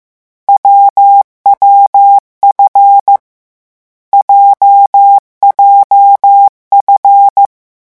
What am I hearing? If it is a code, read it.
Morse code, heard as a keyed tone, about 15 wpm, 800 Hz. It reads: WWF JJF
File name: morse mp3 / lecon7-vitesse 14